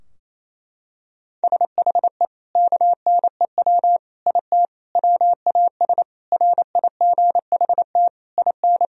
Morse code, heard as a keyed tone, about 28 wpm, 700 Hz. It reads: SHE KNEW IT WAS RIGHT IN
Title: Sentences from Top 1000 Words - Review Part 2 - 28wpm